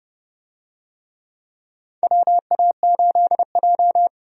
Morse code, heard as a keyed tone, about 30 wpm, 700 Hz. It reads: WA8J
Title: USA Call Signs - 4 Characters - 30wpm